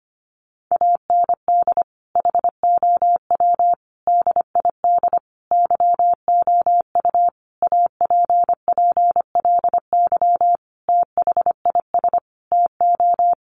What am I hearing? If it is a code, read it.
Morse code, heard as a keyed tone, about 25 wpm, 700 Hz. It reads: AND HOW DID YOU APPLY THIS TO
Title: Sentences from Top 1200 Words 25wpm